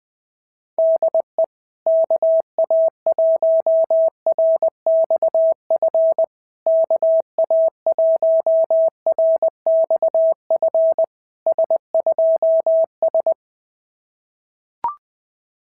Morse code, heard as a keyed tone, about 20 wpm, 650 Hz. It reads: DE KA1RXF KA1RXF S2S
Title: SOTA QSO Practice 20wpm